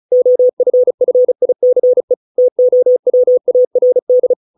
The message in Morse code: OFFICE TOWARD

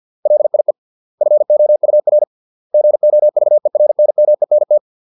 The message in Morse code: LIE FOUR GOVERNMENT

